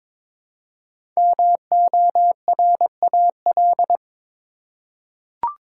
Morse code, heard as a keyed tone, about 22 wpm, 700 Hz. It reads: MORAL
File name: 5 Letter Words - First 3 Letters Given - 22wpm